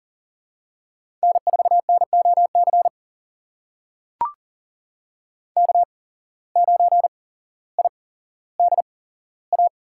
Morse code, heard as a keyed tone, about 40 wpm, spaced as for 17 wpm, 700 Hz. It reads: N4NOC K9IDA
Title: USA Call Signs - 5 Characters - 8x Character Spacing - 40wpm